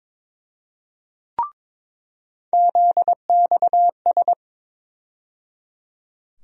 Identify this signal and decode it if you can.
Morse code, heard as a keyed tone, about 22 wpm, 700 Hz. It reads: ZXS
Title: USA Call Sign Postfixes - 3 Letters 22wpm